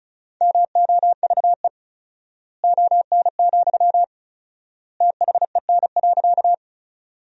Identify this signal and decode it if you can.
Morse code, heard as a keyed tone, about 34 wpm, 700 Hz. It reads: MOVE ON, THEN.
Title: 4x Word-Spacing - Sentences from Top 200 Words - 35wpm